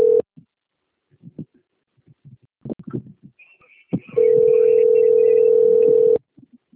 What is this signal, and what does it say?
Ringback tone (North America), 440 + 480 Hz.